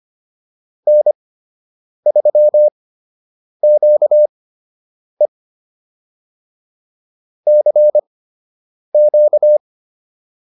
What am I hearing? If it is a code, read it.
Morse code, heard as a keyed tone, about 25 wpm, spaced as for 8 wpm, 600 Hz. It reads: N3QE CQ
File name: QSO 101 08